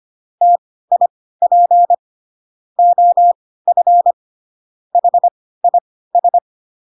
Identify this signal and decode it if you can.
Morse code, heard as a keyed tone, about 25 wpm, spaced as for 16 wpm, 700 Hz. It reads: TIP OF HIS
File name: Pinocchio / Pinocchio ch1 10 25 ik1hge